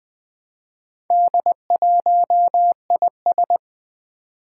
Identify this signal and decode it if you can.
Morse code, heard as a keyed tone, about 20 wpm, 700 Hz. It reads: D1IS
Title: Four Challenging Letter-Number Combinations 20wpm